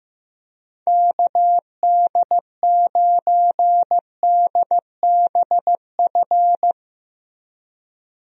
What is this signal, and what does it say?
Morse code, heard as a keyed tone, about 15 wpm, 700 Hz. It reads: KD9DBF
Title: USA Call Signs - 6 Characters - Repeated 2x - 15wpm